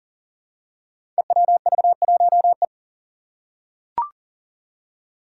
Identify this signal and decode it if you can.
Morse code, heard as a keyed tone, about 40 wpm, 700 Hz. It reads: EWV1E
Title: Five Challenging Letter-Number Combinations 40wpm